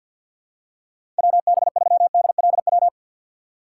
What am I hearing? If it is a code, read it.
Morse code, heard as a keyed tone, about 48 wpm, 700 Hz. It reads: WB3DLW
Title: USA Call Signs - 6 Characters - 4x Character Spacing - 50wpm